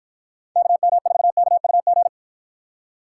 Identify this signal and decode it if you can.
Morse code, heard as a keyed tone, about 53 wpm, 700 Hz. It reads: KM4KUG